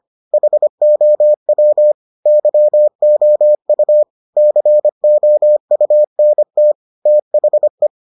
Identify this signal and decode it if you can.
Morse code, heard as a keyed tone, about 25 wpm, 600 Hz. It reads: HOW YOU COUNT THE